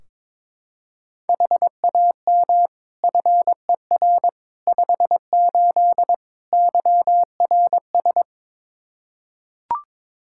Morse code, heard as a keyed tone, about 22 wpm, 700 Hz. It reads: HAM FER 58 YRS
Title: Practice QSO - Exchange 3 - More Info - 22wpm